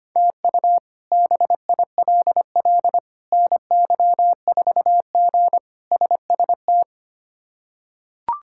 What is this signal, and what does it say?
Morse code, heard as a keyed tone, about 25 wpm, 700 Hz. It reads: TU BILL NY4G SST
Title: K1USN SST QSO Practice - 25wpm